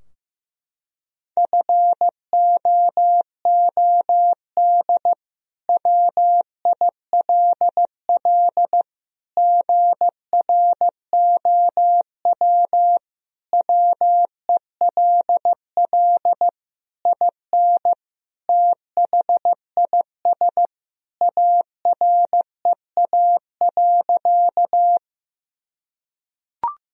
Morse code, heard as a keyed tone, about 15 wpm, 700 Hz. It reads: FOOD WILL GROW WELL IN THIS AREA.
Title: Sentences from Top 1000 Words - Review Part 2 - 15wpm